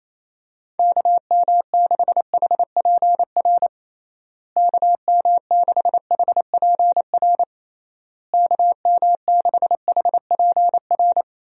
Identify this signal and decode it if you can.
Morse code, heard as a keyed tone, about 28 wpm, 700 Hz. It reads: KM6HPR KM6HPR KM6HPR
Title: USA Call Signs - 6 Characters - Repeated 3x - 28wpm